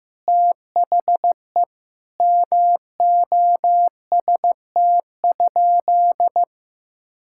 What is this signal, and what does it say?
Morse code, heard as a keyed tone, about 16 wpm, 700 Hz. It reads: THE MOST?
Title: Sentences from Top 1000 Words - Review Part 5 - 15wpm